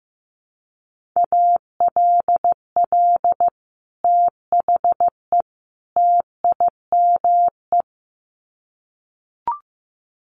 Morse code, heard as a keyed tone, about 15 wpm, 700 Hz. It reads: ALL THE TIME
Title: Sets of 3 Words 15wpm